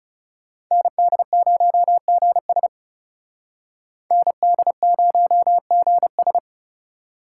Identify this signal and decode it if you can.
Morse code, heard as a keyed tone, about 38 wpm, 700 Hz. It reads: ND0GS ND0GS